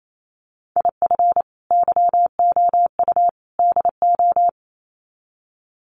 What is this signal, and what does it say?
Morse code, heard as a keyed tone, about 28 wpm, 700 Hz. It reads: IF YOU DO
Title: Sets of 3 Words 28wpm